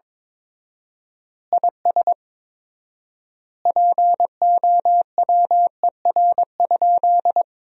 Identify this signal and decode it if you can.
Morse code, heard as a keyed tone, about 21 wpm, 700 Hz. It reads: IS POWER?
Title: 4x Word-Spacing - Sentences from Top 500 Words - 22wpm